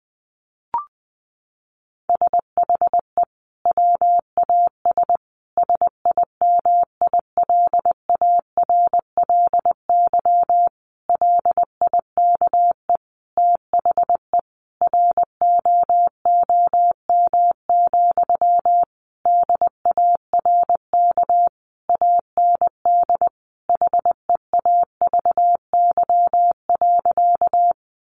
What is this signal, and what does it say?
Morse code, heard as a keyed tone, about 20 wpm, 700 Hz. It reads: SHE WAS SIMILARLY LIKE THE ROOM, DARK AND HEAVY.
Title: Sentences from Top 1400 Words 20wpm